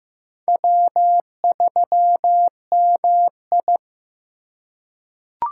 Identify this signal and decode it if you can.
Morse code, heard as a keyed tone, about 15 wpm, 700 Hz. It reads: W3MI
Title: USA Call Signs - 4 Characters - 2x Character Spacing - 15wpm